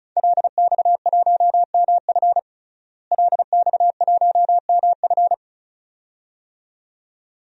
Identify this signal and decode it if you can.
Morse code, heard as a keyed tone, about 36 wpm, 700 Hz. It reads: LX1MF LX1MF
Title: DXCC-100 Call Signs - 5 Characters - Repeated 2x - 35wpm